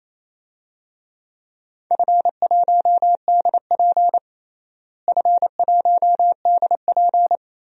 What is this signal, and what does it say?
Morse code, heard as a keyed tone, about 28 wpm, 700 Hz. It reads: F1DP F1DP
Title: DXCC-100 Call Signs - 4 Characters - Repeated 2x - 28wpm